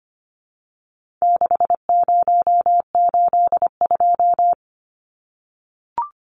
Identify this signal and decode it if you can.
Morse code, heard as a keyed tone, about 25 wpm, 700 Hz. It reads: 6082
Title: Numbers - 4-Digit 25wpm